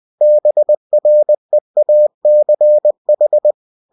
Morse code, heard as a keyed tone, about 20 wpm, 600 Hz. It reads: BREACH